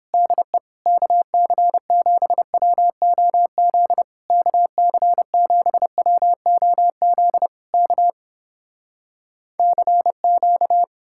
Morse code, heard as a keyed tone, about 31 wpm, 700 Hz. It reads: DE KC7WOZ KC7WOZ K CQ